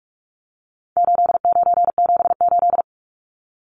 Morse code, heard as a keyed tone, about 44 wpm, 700 Hz. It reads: 8978
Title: Numbers - 4-Digit 45wpm